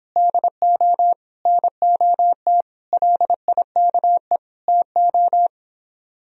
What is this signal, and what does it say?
Morse code, heard as a keyed tone, about 29 wpm, 700 Hz. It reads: DO NOT LIKE TO